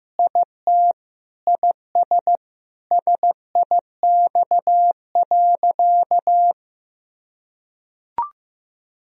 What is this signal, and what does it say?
Morse code, heard as a keyed tone, about 16 wpm, 700 Hz. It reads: IT IS SIX.
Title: Sentences from Top 1000 Words - Review Part 2 - 15wpm